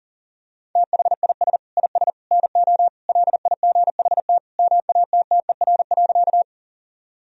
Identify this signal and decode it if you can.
Morse code, heard as a keyed tone, about 40 wpm, 700 Hz. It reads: THIS IS NO LIGHT MATTER.
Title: Sentences from Top 1000 Words - Review Part 1 - 40wpm